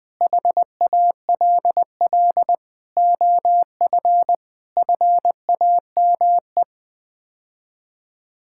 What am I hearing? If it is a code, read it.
Morse code, heard as a keyed tone, about 20 wpm, 700 Hz. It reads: HALL OF FAME
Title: Sets of 6 Words - 20wpm